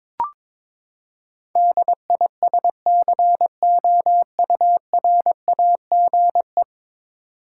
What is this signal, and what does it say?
Morse code, heard as a keyed tone, about 22 wpm, 700 Hz. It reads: DISCOURAGE
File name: practice / 22wpm / Ten Letter Words - 22wpm